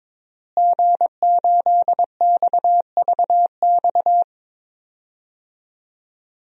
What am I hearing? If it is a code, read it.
Morse code, heard as a keyed tone, about 22 wpm, 700 Hz. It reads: G8XVX
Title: DXCC-100 Call Signs - 5 Characters - Repeated 2x - 22wpm